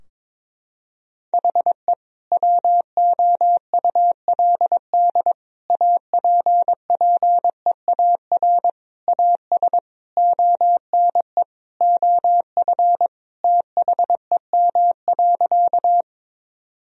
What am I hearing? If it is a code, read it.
Morse code, heard as a keyed tone, about 22 wpm, 700 Hz. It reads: HE WOULD APPEAR AS ONE OF THEM.
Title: Sentences from Top 1000 Words - Review Part 2 - 22wpm